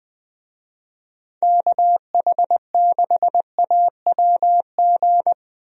Morse code, heard as a keyed tone, about 20 wpm, 700 Hz. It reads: KH6AWG
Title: DXCC-100 Call Signs - 6 Characters - Repeated 2x - 20wpm